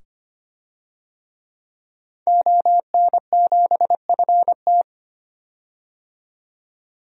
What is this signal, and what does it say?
Morse code, heard as a keyed tone, about 25 wpm, 700 Hz. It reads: ON7FT
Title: DXCC-100 Call Signs - 5 Characters - 25wpm